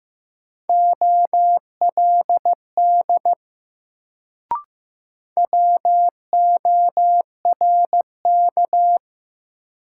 Morse code, heard as a keyed tone, about 15 wpm, 700 Hz. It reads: OLD WORK
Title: Sentences from Top 200 Words - New Word Drill - 15wpm